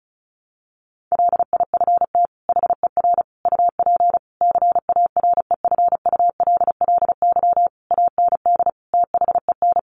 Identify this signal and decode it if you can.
Morse code, heard as a keyed tone, about 34 wpm, 700 Hz. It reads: LIFT HER UP CAREFULLY AND THEN
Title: Sentences from Top 1300 Words 35wpm